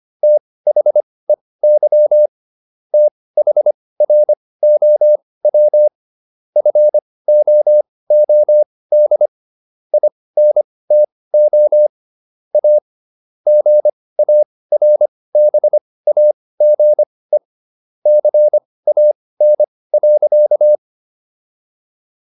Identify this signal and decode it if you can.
Morse code, heard as a keyed tone, about 25 wpm, spaced as for 18 wpm, 600 Hz. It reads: THEY THROW FOOD INTO A GARBAGE CAN.